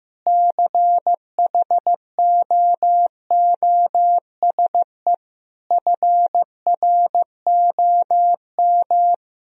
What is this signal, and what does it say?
Morse code, heard as a keyed tone, about 15 wpm, 700 Hz. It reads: CHOOSE FROM
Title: Sentences from Top 1000 Words - Review Part 4 - 15wpm